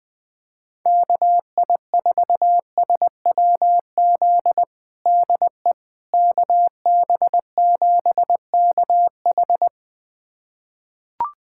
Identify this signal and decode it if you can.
Morse code, heard as a keyed tone, about 20 wpm, 700 Hz. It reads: KI4SWZ DE KB7KH
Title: USA Callsign DE Callsign 20wpm